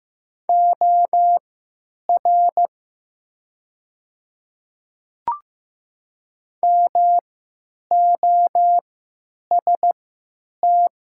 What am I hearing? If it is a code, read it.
Morse code, heard as a keyed tone, about 15 wpm, spaced as for 9 wpm, 700 Hz. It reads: OR MOST
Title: 3x Character Spacing - Top 100 Words - 15wpm